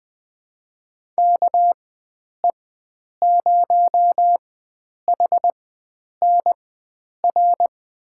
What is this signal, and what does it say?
Morse code, heard as a keyed tone, about 20 wpm, spaced as for 9 wpm, 700 Hz. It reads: KE0HNR
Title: USA Call Signs - 6 Characters - 4x Character Spacing - 20wpm